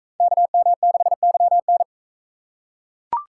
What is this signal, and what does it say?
Morse code, heard as a keyed tone, about 41 wpm, 700 Hz. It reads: KM6YN